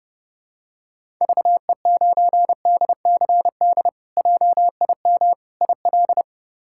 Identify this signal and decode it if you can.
Morse code, heard as a keyed tone, about 30 wpm, 700 Hz. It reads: VE9DCD JIM IL
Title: K1USN SST QSO Practice - 30wpm